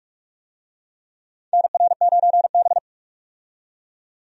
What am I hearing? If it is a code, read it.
Morse code, heard as a keyed tone, about 46 wpm, 700 Hz. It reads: NR9B
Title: USA Call Signs - 4 Characters - 45wpm